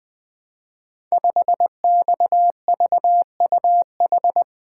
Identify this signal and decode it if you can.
Morse code, heard as a keyed tone, about 20 wpm, 700 Hz. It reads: 5XVUH
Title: Code Groups - 5 Letter:Numbers - 20wpm